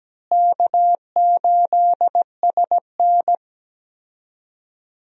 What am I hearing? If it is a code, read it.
Morse code, heard as a keyed tone, about 17 wpm, 700 Hz. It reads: K8SN
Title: USA Call Signs - 4 Characters - Repeated 2x - 17wpm